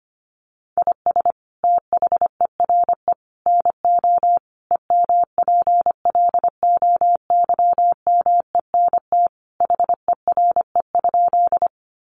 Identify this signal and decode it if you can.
Morse code, heard as a keyed tone, about 25 wpm, 700 Hz. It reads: IS THERE NO EMPLOYMENT HERE?
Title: Sentences from Top 1500 Words 25wpm